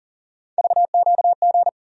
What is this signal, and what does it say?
Morse code, heard as a keyed tone, about 40 wpm, 700 Hz. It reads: VQG
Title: USA Call Sign Postfixes - 3 Letters 40wpm